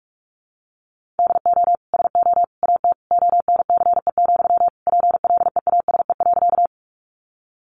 Morse code, heard as a keyed tone, about 44 wpm, 700 Hz. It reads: DO SO AT ONCE, PLEASE.
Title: Sentences from Top 1500 Words 45wpm